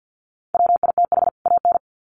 Morse code, heard as a keyed tone, about 50 wpm, 700 Hz. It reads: WITH AN
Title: Sets of 2 Words 50wpm